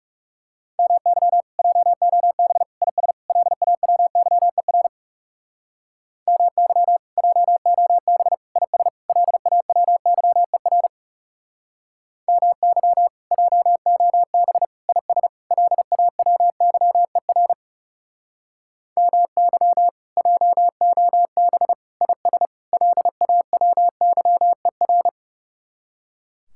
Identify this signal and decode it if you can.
Morse code, heard as a keyed tone, about 35 wpm, 700 Hz. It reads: MY JOB IS LAWYER MY JOB IS LAWYER MY JOB IS LAWYER MY JOB IS LAWYER
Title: Speed-Racing Common QSO Phrases - 30wpm